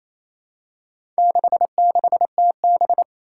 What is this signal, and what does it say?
Morse code, heard as a keyed tone, about 27 wpm, 700 Hz. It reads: 66TB
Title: Four Challenging Letter-Number Combinations 28wpm